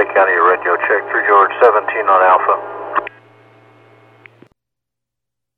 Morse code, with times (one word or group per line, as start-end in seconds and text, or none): none